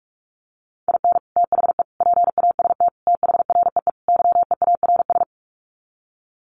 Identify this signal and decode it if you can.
Morse code, heard as a keyed tone, about 44 wpm, 700 Hz. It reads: IN THE PAST THREE YEARS
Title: Sets of 5 Words Encore2 45wpm